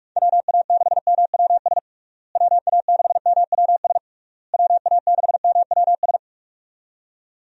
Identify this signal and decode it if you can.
Morse code, heard as a keyed tone, about 46 wpm, 700 Hz. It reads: WA6MWS WA6MWS WA6MWS
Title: USA Call Signs - 6 Characters - Repeated 3x - 45wpm